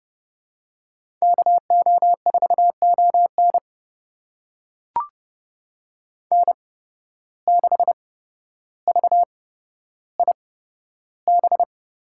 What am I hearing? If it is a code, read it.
Morse code, heard as a keyed tone, about 30 wpm, spaced as for 13 wpm, 700 Hz. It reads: KO4ON N6VIB
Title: USA Call Signs - 5 Characters - 8x Character Spacing - 30wpm